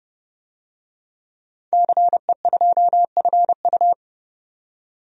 Morse code, heard as a keyed tone, about 30 wpm, 700 Hz. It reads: CE2FU
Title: DXCC-100 Call Signs - 5 Characters - 30wpm